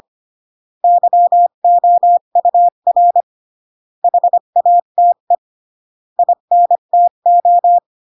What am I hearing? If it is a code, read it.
Morse code, heard as a keyed tone, about 25 wpm, 700 Hz. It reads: YOUR HATE INTO